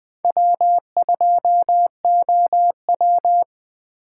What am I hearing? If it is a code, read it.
Morse code, heard as a keyed tone, about 20 wpm, 700 Hz. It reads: W2OW
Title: Speed Racing - USA 1x2 Call Signs 17wpm